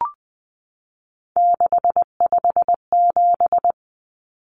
Morse code, heard as a keyed tone, about 20 wpm, 700 Hz. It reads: 657
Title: Common Cut Numbers - 3-Digit 20wpm